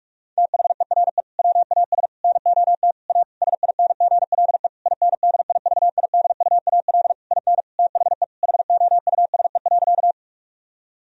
Morse code, heard as a keyed tone, about 46 wpm, 700 Hz. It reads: THERE WAS NOT A SINGLE INDIVIDUAL IN THE HOUSE.
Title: Sentences from Top 1000 Words - Review Part 4 - 45wpm